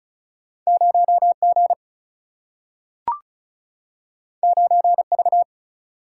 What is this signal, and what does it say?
Morse code, heard as a keyed tone, about 34 wpm, 700 Hz. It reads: 0G 9V
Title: Single-Letter Single-Number 35wpm